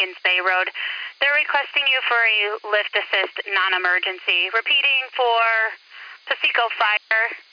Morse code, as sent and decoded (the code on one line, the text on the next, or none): none